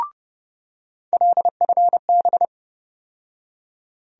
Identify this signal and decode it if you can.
Morse code, heard as a keyed tone, about 32 wpm, 700 Hz. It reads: LFB
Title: USA Call Sign Postfixes - 3 Letters 30wpm